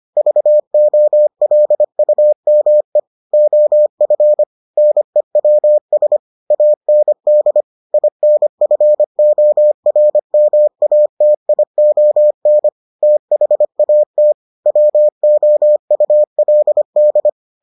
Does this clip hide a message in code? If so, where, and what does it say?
Yes: Morse code from 0.2 to 17.3 s, heard as a keyed tone, about 25 wpm, 600 Hz. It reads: VOLUME OF NEWS AND INFORMATION THAT WOULD